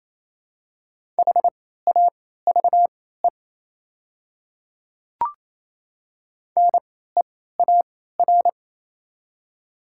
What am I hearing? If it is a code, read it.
Morse code, heard as a keyed tone, about 28 wpm, spaced as for 16 wpm, 700 Hz. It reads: HAVE NEAR
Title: 3x Character Spacing - Top 300 Words - 28wpm